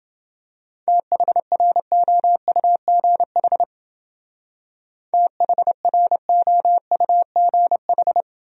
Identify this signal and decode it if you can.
Morse code, heard as a keyed tone, about 29 wpm, 700 Hz. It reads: THROUGH THROUGH